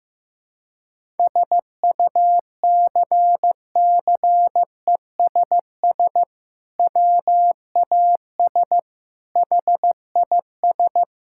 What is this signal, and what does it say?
Morse code, heard as a keyed tone, about 15 wpm, 700 Hz. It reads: SUCCESS WAS HIS
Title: Sentences from Top 1000 Words - Review Part 6 - 15wpm